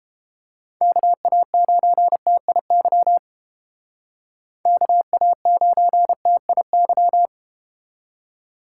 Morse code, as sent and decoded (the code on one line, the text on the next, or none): -.- .- ----. - .. -.-- / -.- .- ----. - .. -.--
KA9TIY KA9TIY